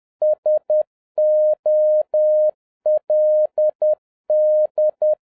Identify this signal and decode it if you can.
Morse code, heard as a keyed tone, about 10 wpm, 610 Hz. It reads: SOLD